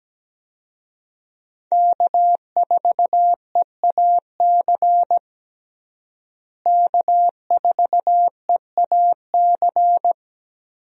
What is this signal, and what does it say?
Morse code, heard as a keyed tone, about 17 wpm, 700 Hz. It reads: K4EAC K4EAC
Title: USA Call Signs - 5 Characters - Repeated 3x - 17wpm